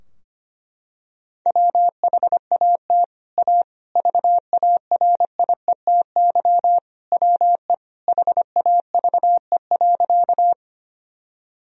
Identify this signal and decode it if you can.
Morse code, heard as a keyed tone, about 25 wpm, 700 Hz. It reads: WHAT A VARIETY WE HAVE.
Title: Sentences from Top 1000 Words - Review Part 4 - 25wpm